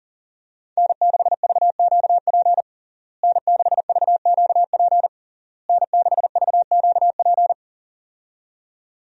Morse code, heard as a keyed tone, about 37 wpm, 700 Hz. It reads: N6VQP N6VQP N6VQP
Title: USA Call Signs - 5 Characters - Repeated 3x - 40wpm